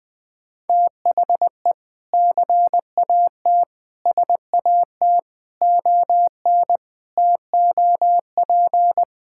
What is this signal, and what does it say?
Morse code, heard as a keyed tone, about 20 wpm, 700 Hz. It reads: THE CAT SAT ON TOP
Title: Sentences with 4-Word Phrases - 20wpm